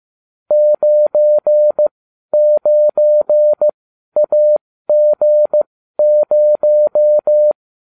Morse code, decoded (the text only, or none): 99AG0